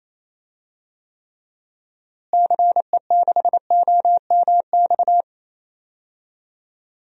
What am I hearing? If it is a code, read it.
Morse code, heard as a keyed tone, about 28 wpm, 700 Hz. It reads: CE6OMX
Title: DXCC-100 Call Signs - 6 Characters - Repeated 2x - 28wpm